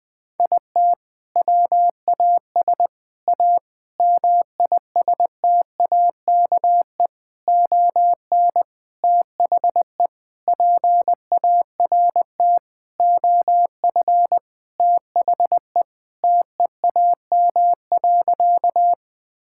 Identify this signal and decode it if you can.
Morse code, heard as a keyed tone, about 20 wpm, 700 Hz. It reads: IT WAS A MISTAKE ON THE PART OF THE TEAM.
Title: Sentences with 4-Word Phrases - Spoken Beginning - 20wpm